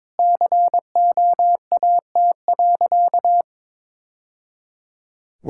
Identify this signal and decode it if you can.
Morse code, heard as a keyed tone, about 21 wpm, 700 Hz. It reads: COAT.
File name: practice / 22wpm / Easy Sentences with Hints - 22wpm